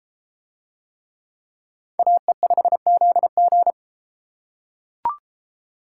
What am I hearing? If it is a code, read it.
Morse code, heard as a keyed tone, about 33 wpm, 700 Hz. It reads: AE5ZG